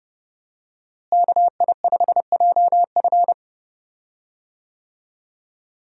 Morse code, heard as a keyed tone, about 30 wpm, 700 Hz. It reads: KI5JF
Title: USA Call Signs - 5 Characters - Repeated 3x - 30wpm